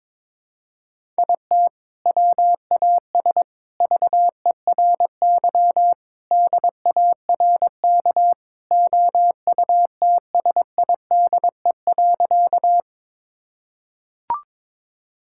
Morse code, heard as a keyed tone, about 22 wpm, 700 Hz. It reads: IT WAS VERY DARK OUTSIDE.
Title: Sentences from Top 1000 Words - Review Part 6 - 22wpm